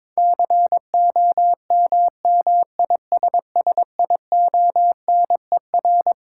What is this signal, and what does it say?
Morse code, heard as a keyed tone, about 22 wpm, 700 Hz. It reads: COMMISSIONER
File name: Twelve Letter Words - 22wpm